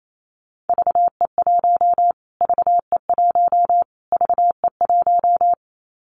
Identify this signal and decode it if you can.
Morse code, heard as a keyed tone, about 28 wpm, 700 Hz. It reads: VE1 VE1 VE1
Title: Callsign Prefixes for Top 25 DXCC - Repeated 3x - 28wpm